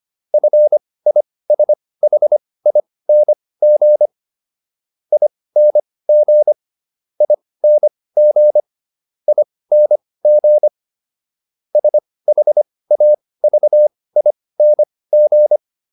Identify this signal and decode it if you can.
Morse code, heard as a keyed tone, about 25 wpm, spaced as for 18 wpm, 600 Hz. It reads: FISHING ING ING ING SHAVING